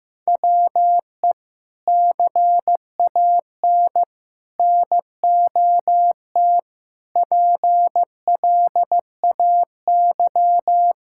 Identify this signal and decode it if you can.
Morse code, heard as a keyed tone, about 15 wpm, 700 Hz. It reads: WE CAN NOT PLAY
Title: Sentences from Top 1000 Words - Review Part 6 - 15wpm